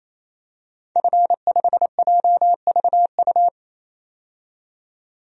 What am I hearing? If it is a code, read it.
Morse code, heard as a keyed tone, about 27 wpm, 700 Hz. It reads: F5JVU
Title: DXCC-100 Call Signs - 5 Characters - 28wpm